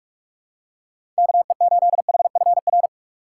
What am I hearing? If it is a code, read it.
Morse code, heard as a keyed tone, about 45 wpm, 700 Hz. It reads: KE8HFR
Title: USA Call Signs - 6 Characters - 8x Character Spacing - 45wpm